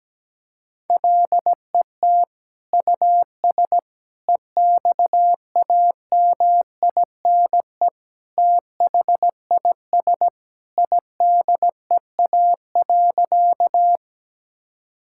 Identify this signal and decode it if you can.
Morse code, heard as a keyed tone, about 17 wpm, 700 Hz. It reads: LET US EXAMINE THIS IDEA.
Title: Sentences from Top 1600 Words 17wpm